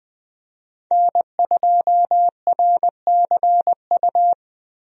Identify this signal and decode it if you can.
Morse code, heard as a keyed tone, about 20 wpm, 700 Hz. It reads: N2RCU
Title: USA Call Signs - 5 Characters - Repeated 2x - 20wpm